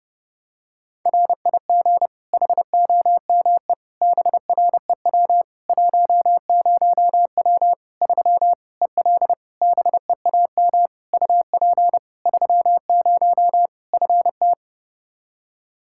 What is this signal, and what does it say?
Morse code, heard as a keyed tone, about 30 wpm, 700 Hz. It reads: RIG HOME BREW 10W 3 EL BEAM UP 30 FT
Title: Practice QSO - Exchange 2 - Info - 30wpm